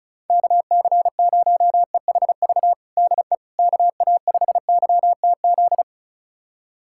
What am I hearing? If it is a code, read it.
Morse code, heard as a keyed tone, about 36 wpm, 700 Hz. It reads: KC0EHV DE KA5YTZ
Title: USA Callsign DE Callsign - 6 Characters - 35wpm